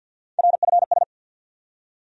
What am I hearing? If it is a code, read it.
Morse code, heard as a keyed tone, about 53 wpm, 700 Hz. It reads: RFS
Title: USA Call Sign Postfixes - 3 Letters 50wpm